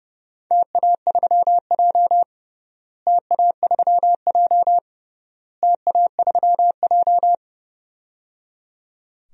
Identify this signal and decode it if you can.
Morse code, heard as a keyed tone, about 30 wpm, 700 Hz. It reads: TA3J TA3J TA3J
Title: DXCC-100 Call Signs - 4 Characters - Repeated 3x - 30wpm